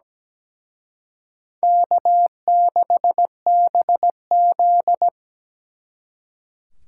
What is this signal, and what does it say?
Morse code, heard as a keyed tone, about 20 wpm, 700 Hz. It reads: K6BZ
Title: Speed Racing - USA 1x2 Call Signs 17wpm